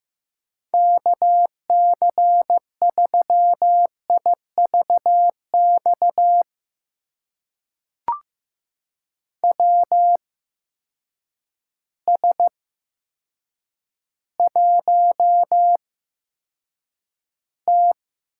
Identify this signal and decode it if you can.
Morse code, heard as a keyed tone, about 15 wpm, spaced as for 8 wpm, 700 Hz. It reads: KC3IVX WS1T
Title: USA Call Signs - 8x Character Spacing - 15wpm